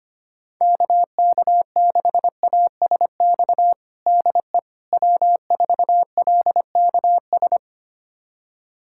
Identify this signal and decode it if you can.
Morse code, heard as a keyed tone, about 25 wpm, 700 Hz. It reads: KK6ASX DE W4LKS
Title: USA Callsign DE Callsign 25wpm